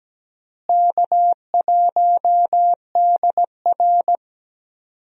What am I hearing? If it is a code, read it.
Morse code, heard as a keyed tone, about 17 wpm, 700 Hz. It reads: K1DR
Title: USA Call Signs - 4 Characters - Repeated 2x - 17wpm